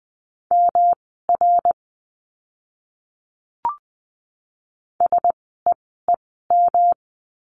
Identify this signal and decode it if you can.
Morse code, heard as a keyed tone, about 20 wpm, spaced as for 15 wpm, 700 Hz. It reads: MR SEEM
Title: 146 Lesson 24f - Letter y - Words 20wpm